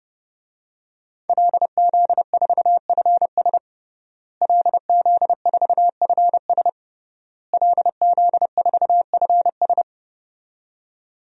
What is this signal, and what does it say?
Morse code, heard as a keyed tone, about 30 wpm, 700 Hz. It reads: LZ4FS LZ4FS LZ4FS
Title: DXCC-100 Call Signs - 5 Characters - Repeated 3x - 30wpm